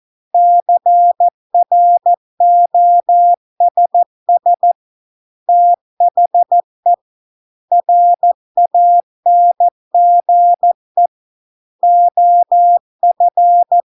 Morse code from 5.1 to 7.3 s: THE